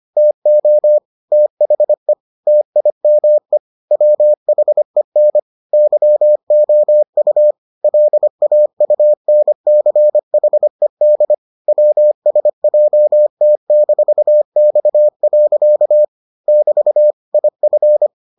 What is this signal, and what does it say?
Morse code, heard as a keyed tone, about 25 wpm, 600 Hz. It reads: TO THE TIME WHEN YOU LAUNCHED WSJT-X. <BT> IF